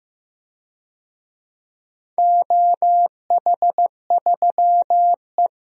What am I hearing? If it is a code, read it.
Morse code, heard as a keyed tone, about 15 wpm, 700 Hz. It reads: OH3E
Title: DXCC-100 Call Signs - 4 Characters - Repeated 3x - 15wpm